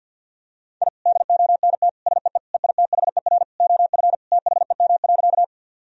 Morse code, heard as a keyed tone, about 50 wpm, 700 Hz. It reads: I DONT SEE EITHER OF THEM.
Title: Sentences from Top 1000 Words - Review Part 4 - 50wpm